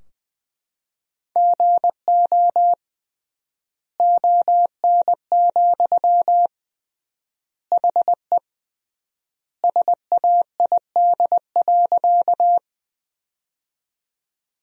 Morse code, heard as a keyed tone, about 20 wpm, 700 Hz. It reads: GO ON, HE SAID.
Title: 3x Word-Spacing - Sentences from Top 100 Words - 20wpm